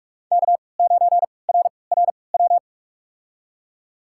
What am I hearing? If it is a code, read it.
Morse code, heard as a keyed tone, about 47 wpm, spaced as for 26 wpm, 700 Hz. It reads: K9RRW
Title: USA Call Signs - 5 Characters - 3x Character Spacing - 45wpm